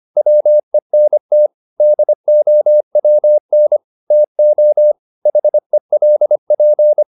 Morse code, heard as a keyed tone, about 25 wpm, 600 Hz. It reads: WENT DOWN TO HELP
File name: the old man and the sea part 1 25wpm